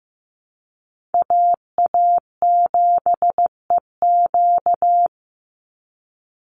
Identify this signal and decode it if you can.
Morse code, heard as a keyed tone, about 15 wpm, 700 Hz. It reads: AA7EQ
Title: NA SOTA Chaser Callsigns 15wpm